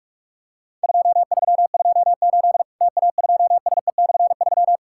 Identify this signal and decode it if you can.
Morse code, heard as a keyed tone, about 43 wpm, 700 Hz. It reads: 2328 TA2SE/3
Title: Worked All Europe DX Contest - QTC Exchange - 45wpm